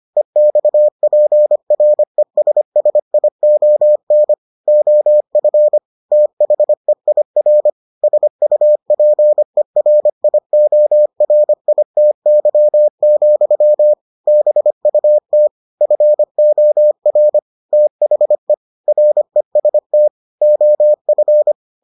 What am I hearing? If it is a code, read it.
Morse code, heard as a keyed tone, about 25 wpm, 600 Hz. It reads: EXPRESSION OF THEIR SUPERIORITY, BUT FOR THE REST OF